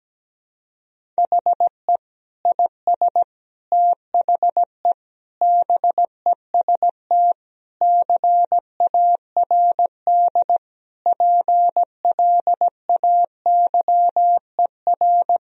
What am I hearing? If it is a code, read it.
Morse code, heard as a keyed tone, about 17 wpm, 700 Hz. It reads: HE IS THE BEST CARD PLAYER
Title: Sentences from Top 1000 Words - Review Part 4 - 17wpm